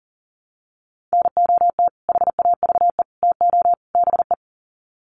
Morse code, heard as a keyed tone, about 40 wpm, 700 Hz. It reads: NOT HAVE TO BE
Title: Sets of 4 Words 40wpm